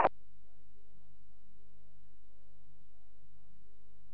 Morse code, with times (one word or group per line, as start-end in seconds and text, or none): none